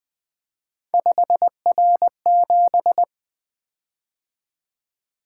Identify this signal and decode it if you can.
Morse code, heard as a keyed tone, about 20 wpm, 700 Hz. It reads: 5R7
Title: Three Letter-Number Combinations - 20wpm